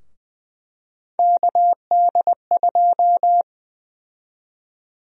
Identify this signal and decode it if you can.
Morse code, heard as a keyed tone, about 20 wpm, 700 Hz. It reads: KD2
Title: USA Call Sign Prefixes - Repeated 2x - 20wpm